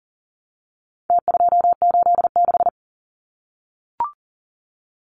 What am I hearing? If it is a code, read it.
Morse code, heard as a keyed tone, about 40 wpm, 700 Hz. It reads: T286
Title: Numbers - Contest Style - 4-Digit 40wpm